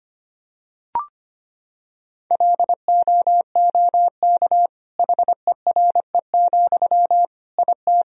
Morse code, heard as a keyed tone, about 25 wpm, 700 Hz. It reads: LOOK HERE, IT
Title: Sentences from Top 1000 Words - Review Part 2 - 25wpm